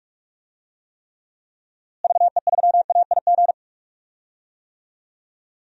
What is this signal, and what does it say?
Morse code, heard as a keyed tone, about 46 wpm, 700 Hz. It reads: VE3AIG
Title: DXCC-100 Call Signs - 6 Characters - Repeated 3x - 45wpm